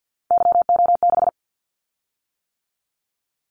Morse code, heard as a keyed tone, about 50 wpm, 700 Hz. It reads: YO6
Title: Callsign Prefixes for Top 25 DXCC - 50wpm